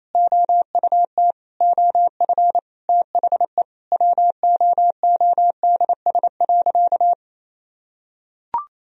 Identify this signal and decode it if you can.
Morse code, heard as a keyed tone, about 28 wpm, 700 Hz. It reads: OUT OF THE WOODS.
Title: Sentences from Top 1000 Words - Review Part 2 - 28wpm